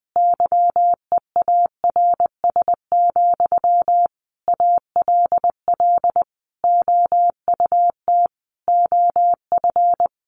Morse code, heard as a keyed tone, about 20 wpm, 700 Hz. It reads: YEARS, ALL OUT OF
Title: A Sound Of Thunder 20wpm